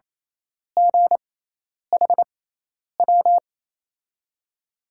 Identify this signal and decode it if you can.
Morse code, heard as a keyed tone, about 28 wpm, spaced as for 10 wpm, 700 Hz. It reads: GHW
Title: USA Call Signs - 6 Characters - 6x Character Spacing - 28wpm